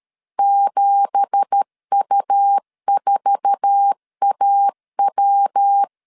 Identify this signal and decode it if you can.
Morse code, heard as a keyed tone, about 13 wpm, 800 Hz. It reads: 7U4AW